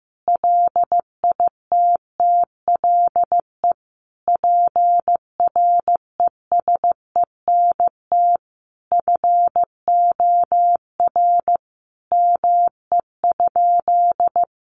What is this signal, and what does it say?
Morse code, heard as a keyed tone, about 15 wpm, 700 Hz. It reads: LITTLE PRESENT FOR ME?
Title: Sentences from Top 800 Words 15wpm